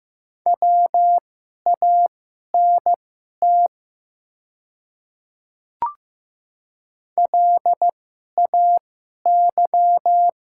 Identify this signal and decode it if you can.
Morse code, heard as a keyed tone, about 15 wpm, spaced as for 11 wpm, 690 Hz. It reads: WANT LAY
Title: 2x Character Spacing - Top 500 Words - 15wpm